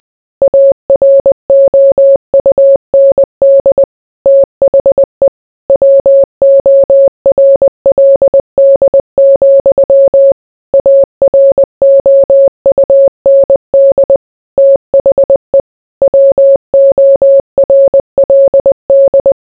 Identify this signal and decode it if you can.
Morse code, heard as a keyed tone, about 20 wpm, 550 Hz. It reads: AROUND THE WORLD, AROUND THE WORLD